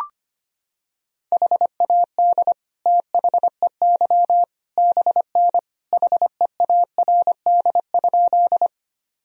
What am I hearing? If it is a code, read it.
Morse code, heard as a keyed tone, about 25 wpm, 700 Hz. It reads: HAD THEY BN HEARD?
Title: Sentences from Top 400 Words and Abbreviations 25wpm